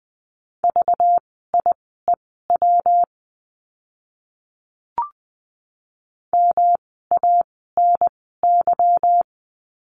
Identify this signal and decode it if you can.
Morse code, heard as a keyed tone, about 20 wpm, spaced as for 14 wpm, 700 Hz. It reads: VIEW MANY
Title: 192 Lesson 31f - Letter v - Words 20wpm